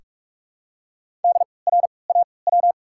Morse code, heard as a keyed tone, about 43 wpm, spaced as for 25 wpm, 700 Hz. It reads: DRAW
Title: 3x Character Spacing - Top 500 Words - 45wpm